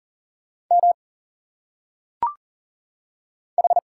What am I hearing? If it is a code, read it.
Morse code, heard as a keyed tone, about 40 wpm, 700 Hz. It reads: M H
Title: Single Letters - Phonetic Pronunciation 40wpm